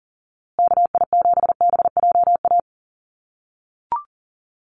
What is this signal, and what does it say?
Morse code, heard as a keyed tone, about 40 wpm, 700 Hz. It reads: KI7BJA
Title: USA Call Signs Repeated 3x 40wpm